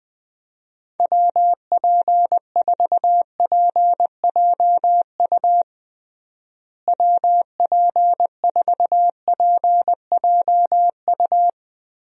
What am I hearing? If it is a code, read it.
Morse code, heard as a keyed tone, about 21 wpm, 700 Hz. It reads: WP4PJU WP4PJU
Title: USA Call Signs - 6 Characters - Repeated 2x - 20wpm